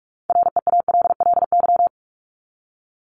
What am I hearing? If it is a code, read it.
Morse code, heard as a keyed tone, about 44 wpm, 700 Hz. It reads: REALLY
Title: Top 501-750 Words 45wpm